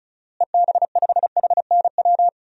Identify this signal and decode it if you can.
Morse code, heard as a keyed tone, about 34 wpm, 700 Hz. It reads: EB5HNW
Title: DXCC-100 Call Signs - 6 Characters - Repeated 2x - 35wpm